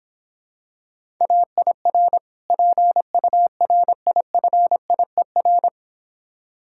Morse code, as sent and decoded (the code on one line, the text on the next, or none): .- .. .-. / .--. ..- .-. .. ..-. .. .-.
AIR PURIFIER